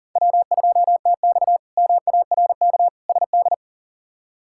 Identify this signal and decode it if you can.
Morse code, heard as a keyed tone, about 40 wpm, 700 Hz. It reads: W2TX MARK SD
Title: K1USN SST QSO Practice - 40wpm